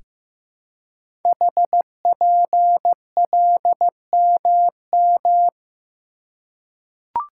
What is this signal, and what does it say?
Morse code, heard as a keyed tone, about 15 wpm, 700 Hz. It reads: HPLMM
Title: Code Groups - 5 Letters - 15wpm